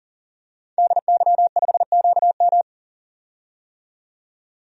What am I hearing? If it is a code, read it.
Morse code, heard as a keyed tone, about 39 wpm, 700 Hz. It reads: DY5QM
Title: DXCC-100 Call Signs - 5 Characters - 40wpm